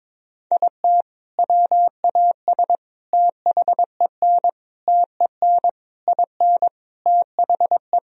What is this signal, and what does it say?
Morse code, heard as a keyed tone, about 22 wpm, 700 Hz. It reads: IT WAS THEN TEN IN THE
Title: Sentences from Top 1000 Words - Review Part 2 - 22wpm